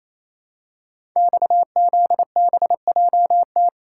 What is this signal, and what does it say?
Morse code, heard as a keyed tone, about 29 wpm, 700 Hz. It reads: XZBJT
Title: Code Groups - 5 Letter:Numbers - 28wpm